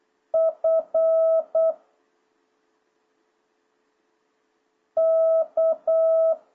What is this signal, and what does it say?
Morse code, heard as a keyed tone, about 8 wpm, 660 Hz. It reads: F K